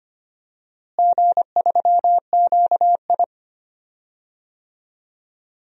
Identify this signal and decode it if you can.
Morse code, heard as a keyed tone, about 25 wpm, 700 Hz. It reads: G3QI
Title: DXCC-100 Call Signs - 4 Characters - Repeated 3x - 25wpm